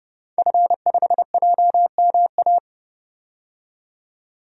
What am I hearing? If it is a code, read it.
Morse code, heard as a keyed tone, about 29 wpm, 700 Hz. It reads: F5JMA
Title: DXCC-100 Call Signs - 5 Characters - 30wpm